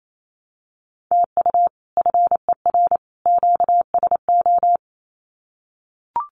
Morse code, heard as a keyed tone, about 28 wpm, 700 Hz. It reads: TU FER QSO
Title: Common QSO Phrases 28wpm